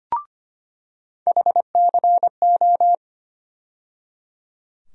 Morse code, heard as a keyed tone, about 25 wpm, 700 Hz. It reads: HCO
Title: USA Call Sign Postfixes - 3 Letters 25wpm